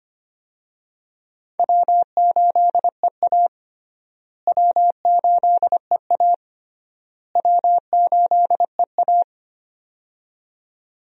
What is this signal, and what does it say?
Morse code, heard as a keyed tone, about 25 wpm, 700 Hz. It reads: W8EA W8EA W8EA
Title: USA Call Signs - 4 Characters - Repeated 3x - 25wpm